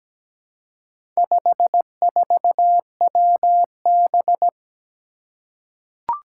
Morse code, heard as a keyed tone, about 17 wpm, 700 Hz. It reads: 54WB
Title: Four Challenging Letter-Number Combinations 17wpm